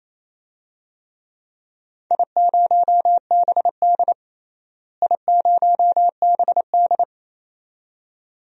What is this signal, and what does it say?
Morse code, heard as a keyed tone, about 27 wpm, 700 Hz. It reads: I0BD I0BD
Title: DXCC-100 Call Signs - 4 Characters - Repeated 2x - 28wpm